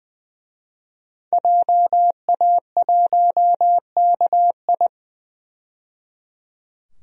Morse code, heard as a keyed tone, about 21 wpm, 700 Hz. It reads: JA1KI
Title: DXCC-100 Call Signs - 5 Characters - 20wpm